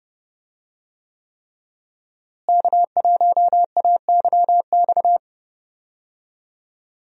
Morse code, heard as a keyed tone, about 30 wpm, 700 Hz. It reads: K1AYX
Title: USA Call Signs - 5 Characters - Repeated 3x - 30wpm